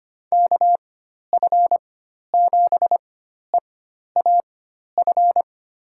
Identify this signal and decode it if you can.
Morse code, heard as a keyed tone, about 25 wpm, spaced as for 12 wpm, 700 Hz. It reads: KF7EAF
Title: USA Call Signs - 4x Character Spacing - 25wpm